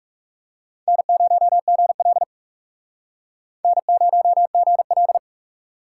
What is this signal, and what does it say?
Morse code, heard as a keyed tone, about 40 wpm, 700 Hz. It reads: N0GL N0GL